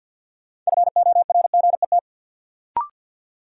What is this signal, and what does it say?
Morse code, heard as a keyed tone, about 52 wpm, 700 Hz. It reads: FORGET
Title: Sentences from Top 1200 Words - New Word Drill - 50wpm